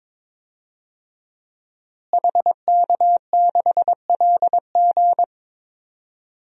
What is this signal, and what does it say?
Morse code, heard as a keyed tone, about 22 wpm, 700 Hz. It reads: HK6LG
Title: DXCC-100 Call Signs - 5 Characters - Repeated 3x - 22wpm